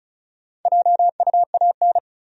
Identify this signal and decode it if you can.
Morse code, heard as a keyed tone, about 36 wpm, 700 Hz. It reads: JUAN
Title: USA Common Names - 35wpm